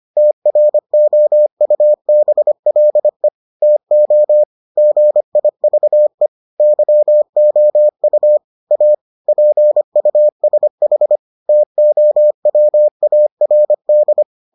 Morse code, heard as a keyed tone, about 25 wpm, 600 Hz. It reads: TROUBLE TO GIVE YOU A PUSH TOWARD